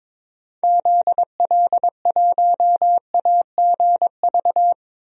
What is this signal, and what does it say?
Morse code, heard as a keyed tone, about 22 wpm, 700 Hz. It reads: ZL1AGV
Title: DXCC-100 Call Signs - 6 Characters - Repeated 3x - 22wpm